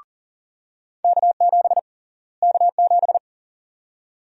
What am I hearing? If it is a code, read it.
Morse code, heard as a keyed tone, about 40 wpm, 700 Hz. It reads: K7 K7
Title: USA Call Sign Prefixes - Repeated 2x - 40wpm